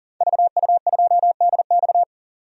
Morse code, heard as a keyed tone, about 40 wpm, 700 Hz. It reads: VU2DX
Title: DXCC-100 Call Signs - 5 Characters - 40wpm